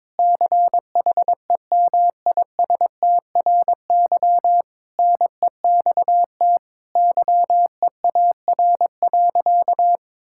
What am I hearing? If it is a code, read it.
Morse code, heard as a keyed tone, about 22 wpm, 700 Hz. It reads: CHEMISTRY NEXT YEAR.
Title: Sentences from Top 1000 Words - Review Part 6 - 22wpm